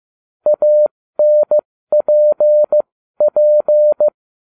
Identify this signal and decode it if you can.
Morse code, heard as a keyed tone, about 15 wpm, spaced as for 13 wpm, 600 Hz. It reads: ANPP